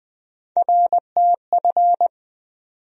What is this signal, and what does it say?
Morse code, heard as a keyed tone, about 20 wpm, 700 Hz. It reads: RTF
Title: USA Call Sign Postfixes - 3 Letters 20wpm